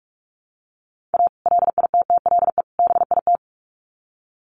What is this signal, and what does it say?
Morse code, heard as a keyed tone, about 44 wpm, 700 Hz. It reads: A LITTLE BIT
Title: Sets of 3 Words 45wpm